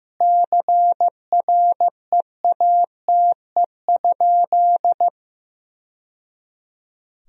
Morse code, heard as a keyed tone, about 16 wpm, 700 Hz. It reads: CREATE?
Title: Sentences from Top 1000 Words - Review Part 3 - 15wpm